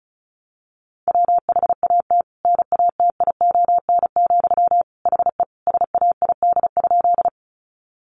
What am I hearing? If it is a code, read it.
Morse code, heard as a keyed tone, about 34 wpm, 700 Hz. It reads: WHAT NATION, HE SAID?
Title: Sentences from Top 900 Words 35wpm